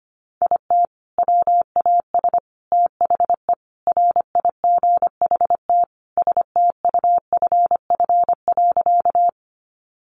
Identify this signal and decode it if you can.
Morse code, heard as a keyed tone, about 25 wpm, 700 Hz. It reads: IT WAS THE RIGHT STUFF.
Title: Sentences from Top 1400 Words 25wpm